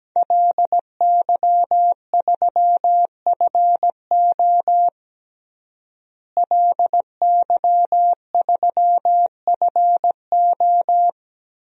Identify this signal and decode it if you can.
Morse code, heard as a keyed tone, about 17 wpm, 700 Hz. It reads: LY3FO LY3FO
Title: DXCC-100 Call Signs - 5 Characters - Repeated 2x - 17wpm